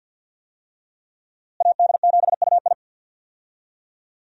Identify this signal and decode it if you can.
Morse code, heard as a keyed tone, about 51 wpm, 700 Hz. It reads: AD7UI
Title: USA Call Signs - 5 Characters - Repeated 3x - 50wpm